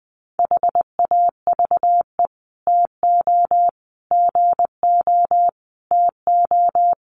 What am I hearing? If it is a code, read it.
Morse code, heard as a keyed tone, about 20 wpm, 700 Hz. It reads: HAVE TO GO TO
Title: Sets of 5 Words Encore 20wpm